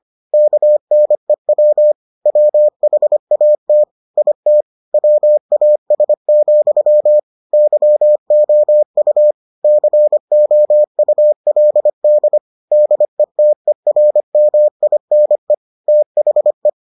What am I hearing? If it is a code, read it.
Morse code, heard as a keyed tone, about 25 wpm, 600 Hz. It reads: KNEW WHAT IT WAS, YOU COULD DETERMINE THE